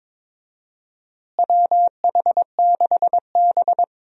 Morse code, heard as a keyed tone, about 22 wpm, 700 Hz. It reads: WH6B
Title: USA Call Signs - 4 Characters - Repeated 2x - 22wpm